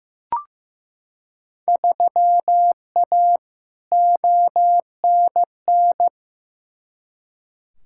Morse code, heard as a keyed tone, about 15 wpm, 700 Hz. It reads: 3A ONN
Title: Field Day Exchanges Encore - 15wpm